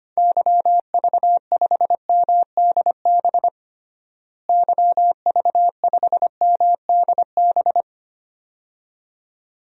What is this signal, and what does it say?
Morse code, heard as a keyed tone, about 25 wpm, 700 Hz. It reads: YV5MDB YV5MDB
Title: DXCC-100 Call Signs - 6 Characters - Repeated 2x - 25wpm